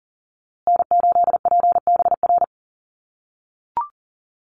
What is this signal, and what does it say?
Morse code, heard as a keyed tone, about 40 wpm, 700 Hz. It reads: N8PBR
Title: USA Call Signs Repeated 2x 40wpm